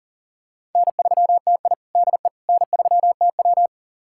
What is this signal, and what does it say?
Morse code, heard as a keyed tone, about 40 wpm, 700 Hz. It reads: N3TI DE N3TW
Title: USA Callsign DE Callsign - 4 Characters - 40wpm